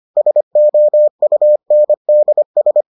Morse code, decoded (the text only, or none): SOUNDS